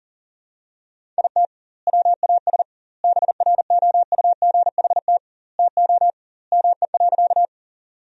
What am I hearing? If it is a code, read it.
Morse code, heard as a keyed tone, about 40 wpm, 700 Hz. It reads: IT WAS BROUGHT TO ME.
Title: 2x Word-Spacing - Sentences from Top 500 Words - 40wpm